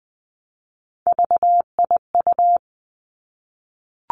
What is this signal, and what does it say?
Morse code, heard as a keyed tone, about 20 wpm, 700 Hz. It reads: VIU
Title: Three Challenging Letter-Number Combinations 20wpm